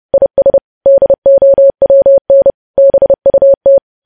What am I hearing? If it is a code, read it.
Morse code, heard as a keyed tone, about 30 wpm, 550 Hz. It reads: IS DOWN BUT
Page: news headlines in morse code at 30 wpm